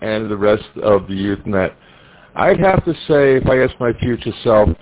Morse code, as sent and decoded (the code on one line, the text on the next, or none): none